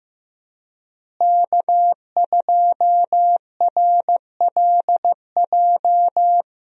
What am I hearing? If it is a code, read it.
Morse code, heard as a keyed tone, about 15 wpm, 700 Hz. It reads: K2RLJ
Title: USA Call Signs - 5 Characters - Repeated 2x - 15wpm